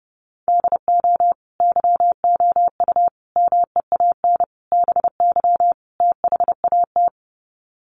Morse code, heard as a keyed tone, about 30 wpm, 700 Hz. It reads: DO YOU MEAN BY THAT
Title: Sets of 5 Words Encore2 30wpm